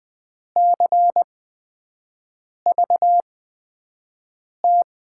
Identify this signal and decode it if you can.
Morse code, heard as a keyed tone, about 20 wpm, spaced as for 5 wpm, 700 Hz. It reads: CVT
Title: USA Call Signs - 8x Character Spacing - 20wpm